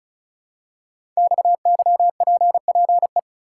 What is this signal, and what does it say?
Morse code, heard as a keyed tone, about 36 wpm, 700 Hz. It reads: XYPPE
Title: Code Groups - 5 Letters - 35wpm